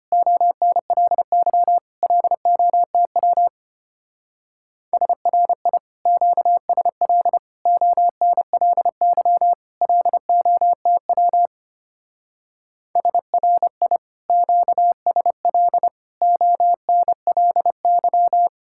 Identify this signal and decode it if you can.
Morse code, heard as a keyed tone, about 30 wpm, 700 Hz. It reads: ONLY LOTW SRI QSL ONLY LOTW SRI QSL ONLY
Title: Speed-Racing Common QSO Phrases - 25wpm